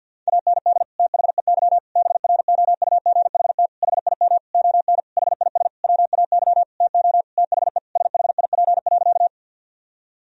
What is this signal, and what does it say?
Morse code, heard as a keyed tone, about 51 wpm, 700 Hz. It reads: AND THEY BROUGHT HIM ON HIS WAY TO THE SHIP.
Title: Sentences from Top 1000 Words - Review Part 2 - 50wpm